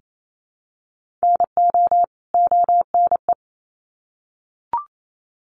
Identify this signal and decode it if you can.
Morse code, heard as a keyed tone, about 28 wpm, 700 Hz. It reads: NO ONE
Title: Sets of 2 Words 28wpm